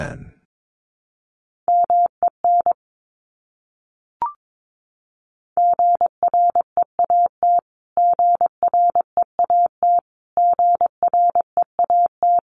Morse code, heard as a keyed tone, about 22 wpm, 700 Hz. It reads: MEN GREAT GREAT GREAT
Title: Top 300 Words Repeated 22wpm